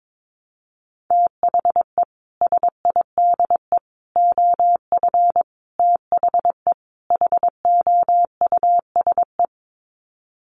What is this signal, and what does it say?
Morse code, heard as a keyed tone, about 22 wpm, 700 Hz. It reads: THE SIDE OF THE HOUSE
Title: Sets of 5 Words Encore2 22wpm